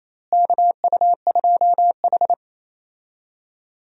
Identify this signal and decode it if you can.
Morse code, heard as a keyed tone, about 27 wpm, 700 Hz. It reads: KU2H
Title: USA Call Signs - 4 Characters - Repeated 3x - 28wpm